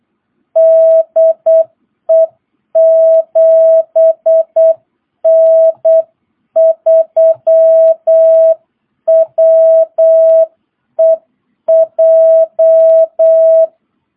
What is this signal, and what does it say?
Morse code, heard as a keyed tone, about 8 wpm, 660 Hz. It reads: DE 7N3WEJ
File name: jh8cbh / 2022cw